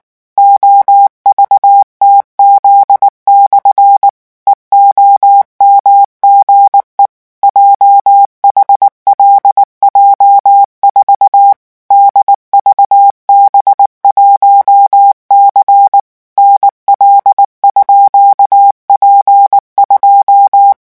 Morse code, heard as a keyed tone, about 19 wpm, 800 Hz. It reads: OVTZ/ EOMGE JHLJ4 DVB1C NL_P2
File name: morse mp3 / revision12-vitesse 18 mots